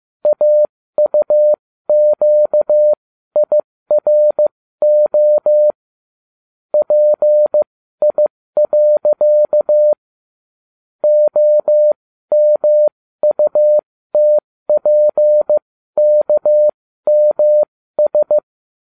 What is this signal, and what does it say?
Morse code, heard as a keyed tone, about 15 wpm, spaced as for 13 wpm, 600 Hz. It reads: AUQIRO PI. OMUTPKMS